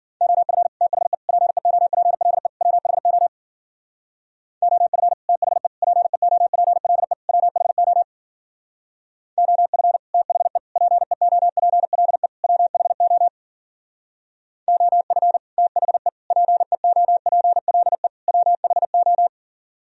Morse code, heard as a keyed tone, about 51 wpm, 700 Hz. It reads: OF THE PEOPLE WHO OF THE PEOPLE WHO OF THE PEOPLE WHO OF THE PEOPLE WHO